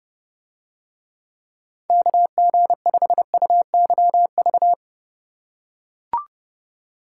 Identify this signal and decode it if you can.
Morse code, heard as a keyed tone, about 29 wpm, 700 Hz. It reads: KG5UYV